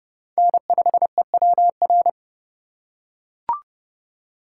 Morse code, heard as a keyed tone, about 30 wpm, 700 Hz. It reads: N5EWR